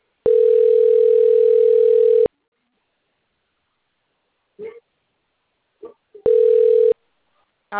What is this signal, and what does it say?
Ringback tone (North America), 440 + 480 Hz.